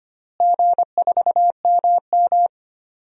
Morse code, heard as a keyed tone, about 25 wpm, 700 Hz. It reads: G4MM
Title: DXCC-100 Call Signs - 4 Characters - Repeated 2x - 25wpm